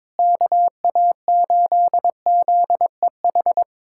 Morse code, heard as a keyed tone, about 22 wpm, 700 Hz. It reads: KA8ZEH
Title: USA Callsign DE Callsign - 6 Characters - 22wpm